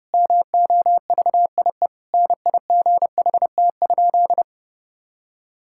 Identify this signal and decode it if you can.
Morse code, heard as a keyed tone, about 32 wpm, 700 Hz. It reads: MOVIE NIGHT?
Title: Sentences from Top 1000 Words - Review Part 5 - 30wpm